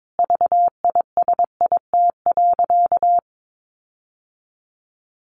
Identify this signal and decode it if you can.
Morse code, heard as a keyed tone, about 22 wpm, 700 Hz. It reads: VISIT.
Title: Sentences from Top 1400 Words 22wpm